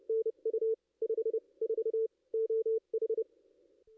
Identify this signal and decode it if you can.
Morse code, heard as a keyed tone, about 29 wpm, spaced as for 25 wpm, 440 Hz. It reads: NU 54 OH
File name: contests / ss-cw-k5tr-op-n5rz -2300